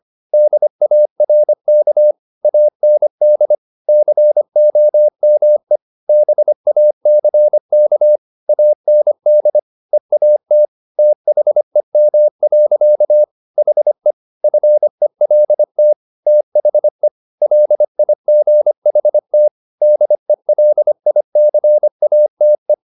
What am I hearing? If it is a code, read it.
Morse code, heard as a keyed tone, about 25 wpm, 600 Hz. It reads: DARK AND COME BACK AND EAT THEM. HE FELT THE LIGHT DELICATE
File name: the old man and the sea part 1 25wpm